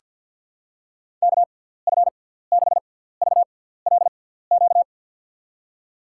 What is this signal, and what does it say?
Morse code, heard as a keyed tone, about 47 wpm, spaced as for 17 wpm, 700 Hz. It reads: KF6VLQ
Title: USA Call Signs - 6 Characters - 6x Character Spacing - 50wpm